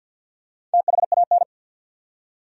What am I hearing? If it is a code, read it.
Morse code, heard as a keyed tone, about 53 wpm, 700 Hz. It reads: THAN
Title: Top 100 Words 50wpm